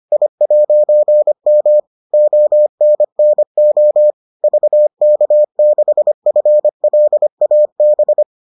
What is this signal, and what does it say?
Morse code, heard as a keyed tone, about 25 wpm, 600 Hz. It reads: I'M ONNO VK6FLAB